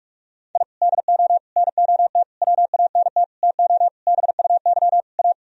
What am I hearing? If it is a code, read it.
Morse code, heard as a keyed tone, about 47 wpm, 700 Hz. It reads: I DO NOT WANT TO BUY A
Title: Sentences from Top 1000 Words - Review Part 4 - 45wpm